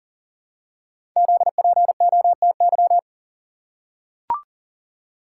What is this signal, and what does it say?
Morse code, heard as a keyed tone, about 38 wpm, 690 Hz. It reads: ZPOTY
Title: Code Groups - 5 Letters - 40wpm